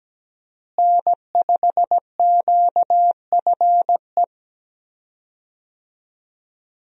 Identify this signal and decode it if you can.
Morse code, heard as a keyed tone, about 17 wpm, 700 Hz. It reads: N5QFE
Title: USA Call Signs - 5 Characters - Repeated 3x - 17wpm